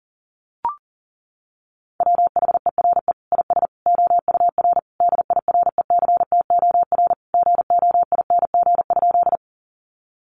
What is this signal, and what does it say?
Morse code, heard as a keyed tone, about 40 wpm, 700 Hz. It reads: WHERE IS OUR DIRECTOR GOING?
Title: Sentences from Top 1300 Words 40wpm